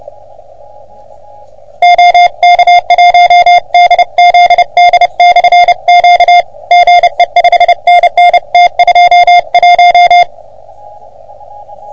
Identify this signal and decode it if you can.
Morse code, heard as a keyed tone, about 28 wpm, 720 Hz. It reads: OK1DZD/Q GE 5NN T21
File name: OM3PA PILUP CUC 388